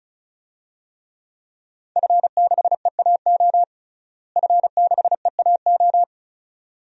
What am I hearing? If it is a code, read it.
Morse code, heard as a keyed tone, about 36 wpm, 700 Hz. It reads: F6EAO F6EAO
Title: DXCC-100 Call Signs - 5 Characters - Repeated 2x - 35wpm